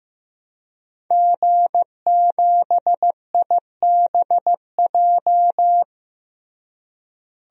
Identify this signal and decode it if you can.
Morse code, heard as a keyed tone, about 15 wpm, 700 Hz. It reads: G7IBJ
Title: DXCC-100 Call Signs - 5 Characters - Repeated 3x - 15wpm